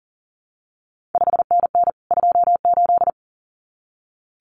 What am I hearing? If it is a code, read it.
Morse code, heard as a keyed tone, about 41 wpm, 700 Hz. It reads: 5NN 28
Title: CQ WW DX Contest Exchange Encore - 40wpm